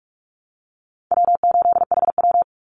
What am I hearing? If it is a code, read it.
Morse code, heard as a keyed tone, about 44 wpm, 700 Hz. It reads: W8HW
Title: Call Signs - 45wpm